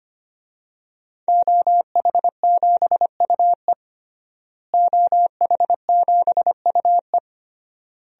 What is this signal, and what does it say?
Morse code, heard as a keyed tone, about 25 wpm, 700 Hz. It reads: OH7UE OH7UE
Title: DXCC-100 Call Signs - 5 Characters - Repeated 3x - 25wpm